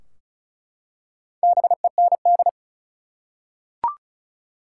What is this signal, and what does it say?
Morse code, heard as a keyed tone, about 34 wpm, 700 Hz. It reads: BEND
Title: Four Letter Words - 35wpm